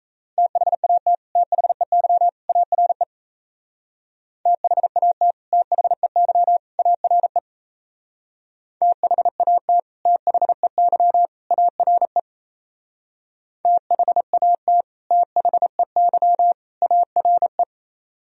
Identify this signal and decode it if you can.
Morse code, heard as a keyed tone, about 33 wpm, 700 Hz. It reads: THAT THEY ARE THAT THEY ARE THAT THEY ARE THAT THEY ARE
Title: Speed-Racing - Sets of 3 Words 28wpm